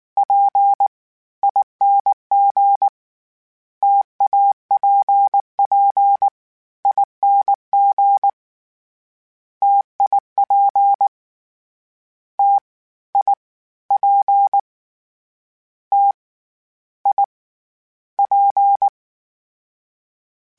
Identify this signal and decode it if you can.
Morse code, heard as a keyed tone, about 19 wpm, 800 Hz. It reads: P ING TAPP ING TIP T I P T I P
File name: Difficult Combos / Difficult Combos-mp3s / T AOEI @18